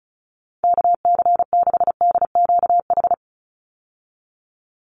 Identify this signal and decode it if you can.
Morse code, heard as a keyed tone, about 34 wpm, 700 Hz. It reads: KC6DQH
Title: USA Call Signs 35wpm